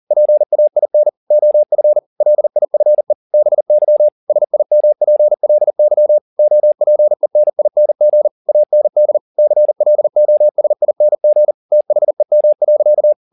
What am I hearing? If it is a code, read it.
Morse code, heard as a keyed tone, about 40 wpm, 600 Hz. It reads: PAIN OF LIFE BY SIMPLY OPENING AND CLOSING THEM.
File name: the old man and the sea 2 40wpm